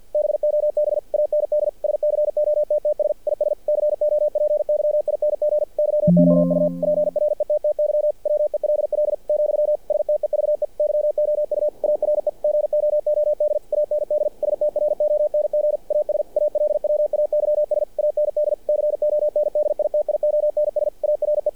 Morse code, heard as a keyed tone, about 48 wpm, 600 Hz. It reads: BOB AND SCOTTS IS COPYING YOU PRETTY WELL , STEVE YOU ARE GOOD AND STRONG AS ALWAYS AND CONDITIONS ARE
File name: archived / CRSnet